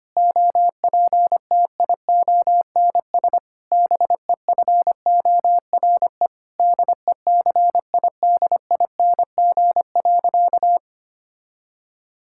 Morse code, heard as a keyed tone, about 25 wpm, 700 Hz. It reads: OPTIONS BEFORE DECIDING.
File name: practice / 25wpm / Difficult Sentences with Hints - 25wpm